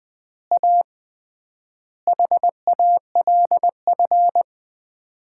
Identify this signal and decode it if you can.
Morse code, heard as a keyed tone, about 21 wpm, 700 Hz. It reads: A HALF
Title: 3x Word-Spacing - Sets of 5 Words - 20wpm